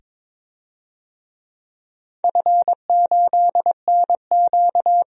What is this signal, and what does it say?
Morse code, heard as a keyed tone, about 22 wpm, 700 Hz. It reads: F8NQ
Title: DXCC-100 Call Signs - 4 Characters - 22wpm